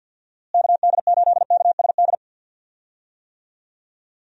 Morse code, heard as a keyed tone, about 47 wpm, 700 Hz. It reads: KD8KSD
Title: USA Call Signs - 6 Characters - Repeated 2x - 50wpm